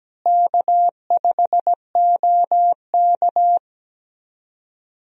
Morse code, heard as a keyed tone, about 17 wpm, 700 Hz. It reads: K5OK
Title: USA Call Signs - 4 Characters - Repeated 2x - 17wpm